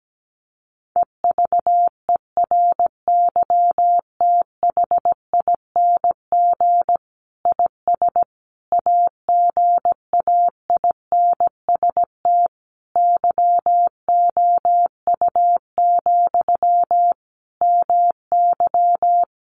Morse code, heard as a keyed tone, about 17 wpm, 700 Hz. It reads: EVERYTHING IS AGAINST YOU, MY
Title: Sentences from Top 700 Words 17wpm